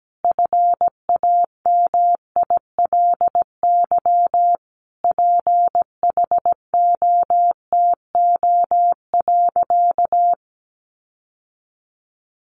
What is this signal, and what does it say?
Morse code, heard as a keyed tone, about 17 wpm, 700 Hz. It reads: FAMILY PHOTO.
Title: Sentences from Top 1200 Words 17wpm